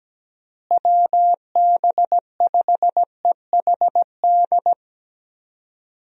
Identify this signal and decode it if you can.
Morse code, heard as a keyed tone, about 17 wpm, 700 Hz. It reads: WB5EHD
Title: USA Call Signs - 6 Characters - Repeated 2x - 17wpm